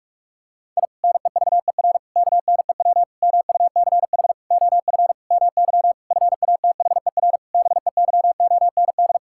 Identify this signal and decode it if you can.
Morse code, heard as a keyed tone, about 46 wpm, 700 Hz. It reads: I NEVER KNEW MUCH OF MY FATHER BEYOND
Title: Sentences from Top 1000 Words - Review Part 6 - 45wpm